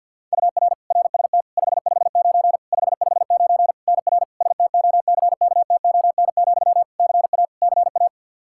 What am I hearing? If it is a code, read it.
Morse code, heard as a keyed tone, about 52 wpm, 700 Hz. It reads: UR RST 559 559 NR STOCKTON, CA CA
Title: Practice QSO - Exchange 1 - Beginning - 50wpm